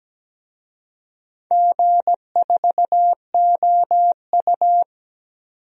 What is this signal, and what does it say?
Morse code, heard as a keyed tone, about 17 wpm, 700 Hz. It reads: G4OU
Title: DXCC-100 Call Signs - 4 Characters - 17wpm